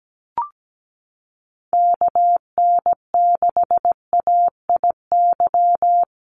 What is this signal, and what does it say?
Morse code, heard as a keyed tone, about 17 wpm, 700 Hz. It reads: KN6AIY
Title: USA Call Signs Repeated 2x 17wpm